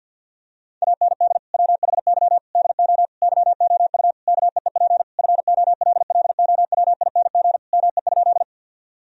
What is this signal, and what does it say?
Morse code, heard as a keyed tone, about 50 wpm, 700 Hz. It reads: AND WHY DO YOU KEEP FOLLOWING ME?
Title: Sentences from Top 1000 Words - Review Part 4 - 50wpm